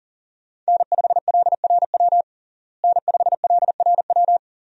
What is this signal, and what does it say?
Morse code, heard as a keyed tone, about 40 wpm, 700 Hz. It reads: N5LRW N5LRW
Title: USA Call Signs - 5 Characters - Repeated 2x - 40wpm